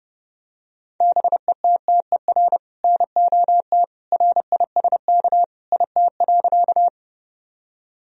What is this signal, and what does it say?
Morse code, heard as a keyed tone, about 32 wpm, 700 Hz. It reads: BETTER NOT RISK IT.
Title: Sentences from Top 1000 Words - Review Part 4 - 30wpm